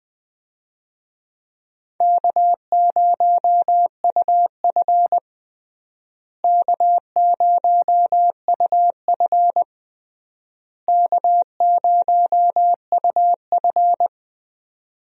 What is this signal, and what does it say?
Morse code, heard as a keyed tone, about 20 wpm, 700 Hz. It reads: K0UF K0UF K0UF
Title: USA Call Signs - 4 Characters - Repeated 3x - 20wpm